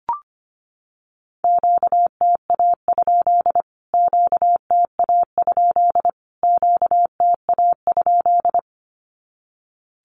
Morse code, heard as a keyed tone, about 25 wpm, 700 Hz. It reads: QTA? QTA? QTA?
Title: Q-Codes Repeated 3x 25wpm